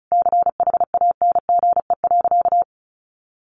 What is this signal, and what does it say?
Morse code, heard as a keyed tone, about 34 wpm, 700 Hz. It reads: CHANGE.
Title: Sentences from Top 1500 Words 35wpm